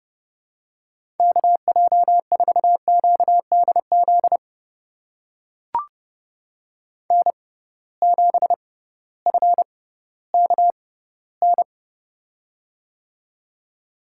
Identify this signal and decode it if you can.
Morse code, heard as a keyed tone, about 32 wpm, spaced as for 17 wpm, 700 Hz. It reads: KJ4QDZ N7FKN
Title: USA Call Signs - 6x Character Spacing - 30wpm